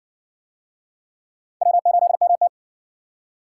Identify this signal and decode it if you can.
Morse code, heard as a keyed tone, about 60 wpm, 700 Hz. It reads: W8MT